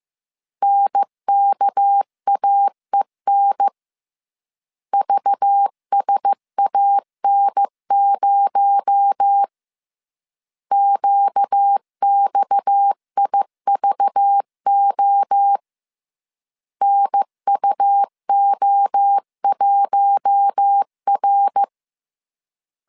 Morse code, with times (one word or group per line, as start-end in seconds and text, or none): 0.6-3.7 NKAEN
4.9-9.4 VSAN0
10.7-15.6 QXIVO
16.8-21.6 NUO1R